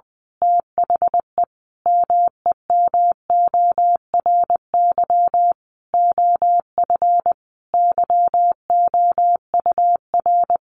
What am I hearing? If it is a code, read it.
Morse code, heard as a keyed tone, about 20 wpm, 700 Hz. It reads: THE MEMORY OF YOUR
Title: Sentences from Top 1300 Words 20wpm